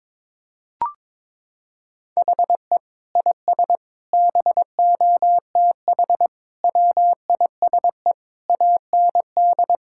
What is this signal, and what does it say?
Morse code, heard as a keyed tone, about 22 wpm, 700 Hz. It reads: HE IS BOTH WISE AND
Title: Sentences from Top 1600 Words 22wpm